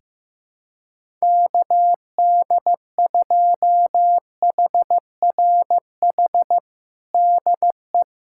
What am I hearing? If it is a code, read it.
Morse code, heard as a keyed tone, about 15 wpm, 700 Hz. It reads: KD2HRH DE
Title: USA Callsign DE Callsign 15wpm